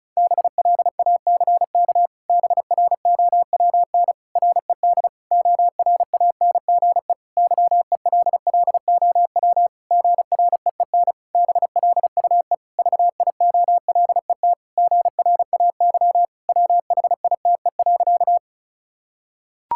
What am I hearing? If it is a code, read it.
Morse code, heard as a keyed tone, about 36 wpm, 700 Hz. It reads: BLACK BROWN RED ORANGE YELLOW GREEN BLUE VIOLET GRAY WHITE.
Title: Pangram Challenge v2 - 35wpm